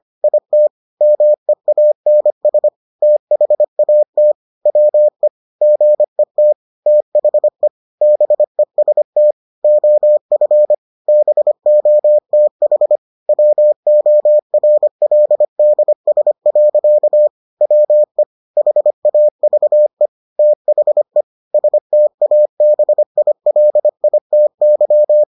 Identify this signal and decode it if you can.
Morse code, heard as a keyed tone, about 25 wpm, 600 Hz. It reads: IT MEANS THAT WE GET THE BEST OF BOTH WORLDS. WE HAVE THE STABILITY